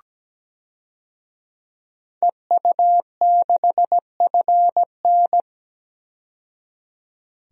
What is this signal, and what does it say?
Morse code, heard as a keyed tone, about 17 wpm, 700 Hz. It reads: EU6FN
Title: DXCC-100 Call Signs - 5 Characters - 17wpm